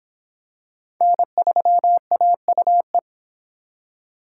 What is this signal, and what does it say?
Morse code, heard as a keyed tone, about 24 wpm, 700 Hz. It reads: N3AUE